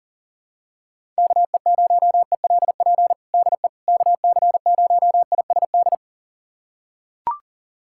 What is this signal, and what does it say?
Morse code, heard as a keyed tone, about 40 wpm, 700 Hz. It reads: KE0ELP DE KC0ISD
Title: USA Callsign DE Callsign - 6 Characters - 40wpm